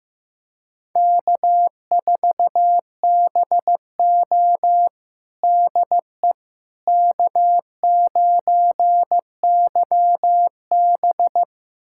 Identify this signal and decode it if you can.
Morse code, heard as a keyed tone, about 15 wpm, 700 Hz. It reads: K4BO DE K9YB
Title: USA Callsign DE Callsign - 4 Characters - 15wpm